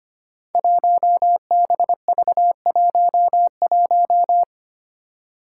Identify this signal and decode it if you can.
Morse code, heard as a keyed tone, about 25 wpm, 700 Hz. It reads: 1BV11
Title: Code Groups - 5 Letter:Numbers - 25wpm